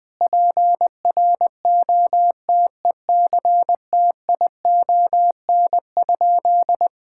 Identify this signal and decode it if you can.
Morse code, heard as a keyed tone, about 21 wpm, 700 Hz. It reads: PROTECTION?
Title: Sentences from Top 1600 Words 20wpm